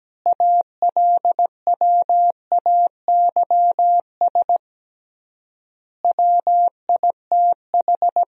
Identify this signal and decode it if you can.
Morse code, heard as a keyed tone, about 17 wpm, 700 Hz. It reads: ALWAYS WITH
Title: 3x Word-Spacing - Sentences from Top 400 Words - 17wpm